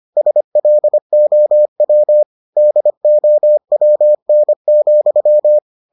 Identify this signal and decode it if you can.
Morse code, heard as a keyed tone, about 25 wpm, 600 Hz. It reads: SLOW DOWN,